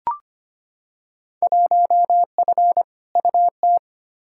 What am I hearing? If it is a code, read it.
Morse code, heard as a keyed tone, about 25 wpm, 700 Hz. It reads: 1F UT
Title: Field Day Exchanges Encore - 25wpm